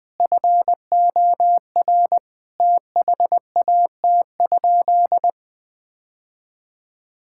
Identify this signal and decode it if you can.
Morse code, heard as a keyed tone, about 20 wpm, 700 Hz. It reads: FOR THAT?
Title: Sentences from Top 1000 Words - Review Part 6 - 20wpm